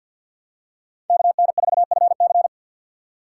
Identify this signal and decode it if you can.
Morse code, heard as a keyed tone, about 49 wpm, 700 Hz. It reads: KN4FC
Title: USA Call Signs - 5 Characters - 6x Character Spacing - 50wpm